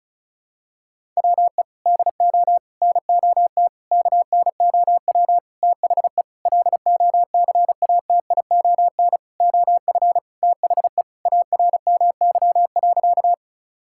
Morse code, heard as a keyed tone, about 36 wpm, 700 Hz. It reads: WE DO NOT KNOW THE LOCATION OF THE ARMY.
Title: Sentences from Top 1000 Words - Review Part 6 - 35wpm